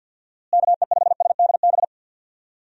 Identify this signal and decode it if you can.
Morse code, heard as a keyed tone, about 51 wpm, 700 Hz. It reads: KE5SDB
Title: USA Call Signs - 6 Characters - Repeated 2x - 50wpm